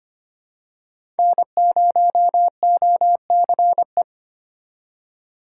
Morse code, heard as a keyed tone, about 25 wpm, 700 Hz. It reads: N0OCE
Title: USA Call Signs - 4x Character Spacing - 25wpm